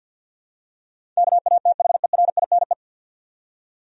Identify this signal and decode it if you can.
Morse code, heard as a keyed tone, about 46 wpm, 700 Hz. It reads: KATHERINE
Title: USA Common Names - 50wpm